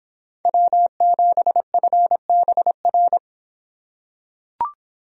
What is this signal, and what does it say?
Morse code, heard as a keyed tone, about 25 wpm, 700 Hz. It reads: W7FBR